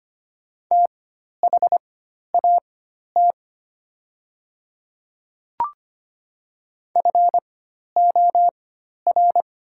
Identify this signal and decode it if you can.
Morse code, heard as a keyed tone, about 25 wpm, spaced as for 12 wpm, 700 Hz. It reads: THAT FOR
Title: 4x Character Spacing - Top 100 Words - 25wpm